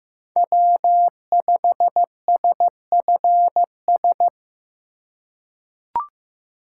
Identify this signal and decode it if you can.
Morse code, heard as a keyed tone, about 15 wpm, 700 Hz. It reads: W5SFS
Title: USA Call Signs - 5 Characters - 3x Character Spacing - 15wpm